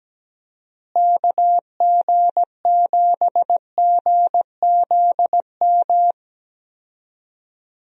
Morse code, heard as a keyed tone, about 17 wpm, 700 Hz. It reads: KG7GZM
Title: USA Call Signs - 6 Characters - Repeated 2x - 17wpm